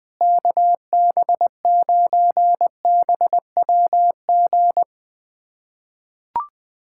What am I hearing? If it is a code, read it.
Morse code, heard as a keyed tone, about 20 wpm, 700 Hz. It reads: KB9BWG
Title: USA Callsign DE Callsign - 6 Characters - 20wpm